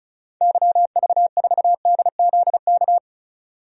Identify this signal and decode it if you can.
Morse code, heard as a keyed tone, about 36 wpm, 700 Hz. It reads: YV4DZK
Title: DXCC-100 Call Signs - 6 Characters - 35wpm